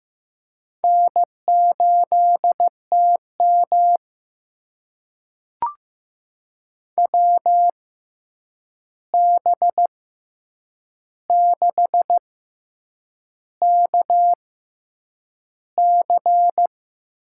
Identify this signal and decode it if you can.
Morse code, heard as a keyed tone, about 16 wpm, spaced as for 7 wpm, 700 Hz. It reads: N8TM WB6KC
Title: USA Call Signs - 6x Character Spacing - 15wpm